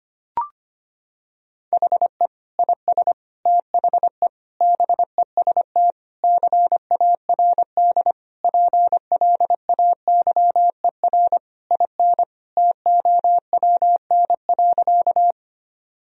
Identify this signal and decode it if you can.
Morse code, heard as a keyed tone, about 25 wpm, 700 Hz. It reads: HE IS THE BEST CARD PLAYER IN TOWN.
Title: Sentences from Top 1000 Words - Review Part 4 - 25wpm